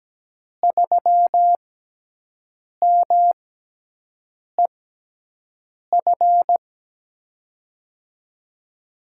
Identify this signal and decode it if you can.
Morse code, heard as a keyed tone, about 17 wpm, spaced as for 6 wpm, 700 Hz. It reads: 3MEF
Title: USA Call Signs - 5 Characters - 6x Character Spacing - 17wpm